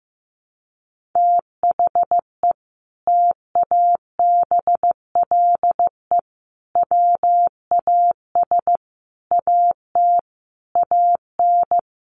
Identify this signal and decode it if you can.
Morse code, heard as a keyed tone, about 15 wpm, 700 Hz. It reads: THE TABLE WAS AT AN
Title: Sentences from Top 1500 Words 15wpm